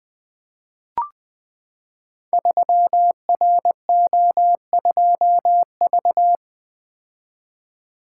Morse code, heard as a keyed tone, about 20 wpm, 700 Hz. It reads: 3RO2V
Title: Code Groups - 5 Letter:Numbers - 20wpm